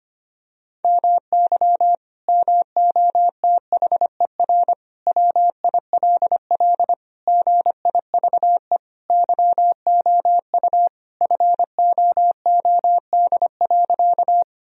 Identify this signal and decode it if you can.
Morse code, heard as a keyed tone, about 25 wpm, 700 Hz. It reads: MY MOTHER WILL GIVE YOU FOOD.
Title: Sentences from Top 1000 Words - Review Part 1 - 25wpm